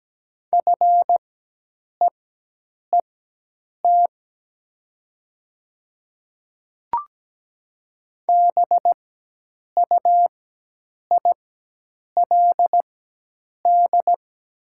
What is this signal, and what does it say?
Morse code, heard as a keyed tone, about 17 wpm, spaced as for 8 wpm, 700 Hz. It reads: FEET BUILD
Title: 4x Character Spacing - Top 500 Words - 17wpm